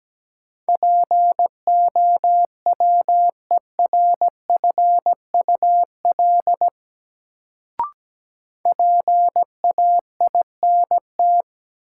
Morse code, heard as a keyed tone, about 17 wpm, 700 Hz. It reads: POWERFUL PAINT
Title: Sentences from Top 1000 Words - Review Part 6 - New Word Drill - 17wpm